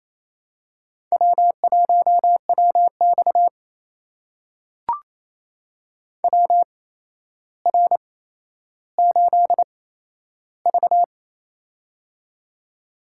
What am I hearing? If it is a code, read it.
Morse code, heard as a keyed tone, about 27 wpm, spaced as for 12 wpm, 700 Hz. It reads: W1WX WR8V
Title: USA Call Signs - 4 Characters - 8x Character Spacing - 28wpm